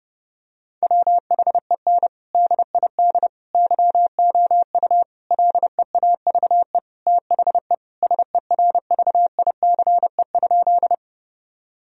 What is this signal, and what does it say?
Morse code, heard as a keyed tone, about 32 wpm, 700 Hz. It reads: WHEN DID YOU LEAVE THE SERVICE?
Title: Sentences from Top 1000 Words - Review Part 4 - 30wpm